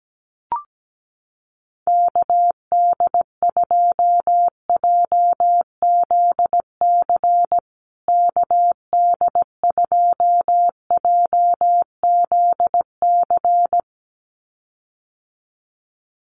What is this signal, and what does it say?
Morse code, heard as a keyed tone, about 17 wpm, 700 Hz. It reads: KD2JZC KD2JZC
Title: USA Call Signs Repeated 2x 17wpm